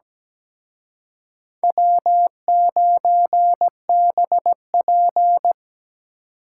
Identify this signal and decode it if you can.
Morse code, heard as a keyed tone, about 20 wpm, 700 Hz. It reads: W9BP
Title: Speed Racing - USA 1x2 Call Signs 17wpm